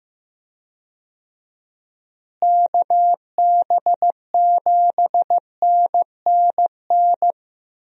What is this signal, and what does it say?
Morse code, heard as a keyed tone, about 15 wpm, 700 Hz. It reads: KB7NNN
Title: USA Call Signs - 6 Characters - Repeated 2x - 15wpm